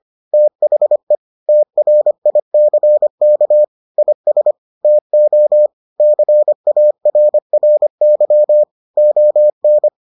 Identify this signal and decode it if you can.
Morse code, heard as a keyed tone, about 25 wpm, 600 Hz. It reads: THE TRICK IS TO CARRY ON